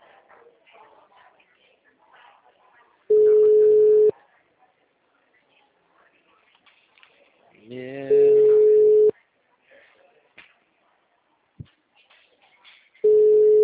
Ringback tone (Europe, 425 Hz).
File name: out- -2013- - - .66085